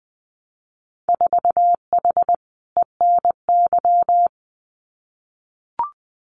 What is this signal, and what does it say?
Morse code, heard as a keyed tone, about 20 wpm, 700 Hz. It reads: 4H ENY
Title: Winter Field Day Exchanges Encore 20wpm